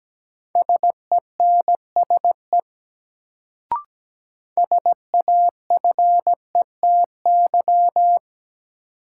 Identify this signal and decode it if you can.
Morse code, heard as a keyed tone, about 17 wpm, 700 Hz. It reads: SENSE SAFETY
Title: Sentences from Top 700 Words - New Word Drill - 17wpm